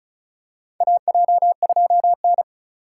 Morse code, heard as a keyed tone, about 36 wpm, 700 Hz. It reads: AJ2N
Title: USA Call Signs - 4 Characters - 2x Character Spacing - 35wpm